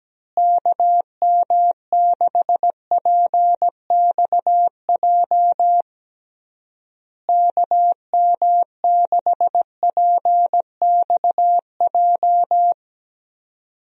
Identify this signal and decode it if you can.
Morse code, heard as a keyed tone, about 17 wpm, 700 Hz. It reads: KM6PXJ KM6PXJ
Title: USA Call Signs - 6 Characters - Repeated 3x - 17wpm